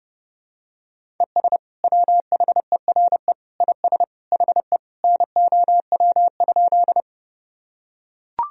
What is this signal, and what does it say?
Morse code, heard as a keyed tone, about 30 wpm, 700 Hz. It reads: ES WHERE IS HE NOW?
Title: Sentences from Top 200 Words and Abbreviations 30wpm